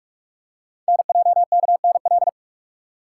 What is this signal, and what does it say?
Morse code, heard as a keyed tone, about 47 wpm, 700 Hz. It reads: NJKNL
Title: Code Groups - 5 Letter:Numbers - 45wpm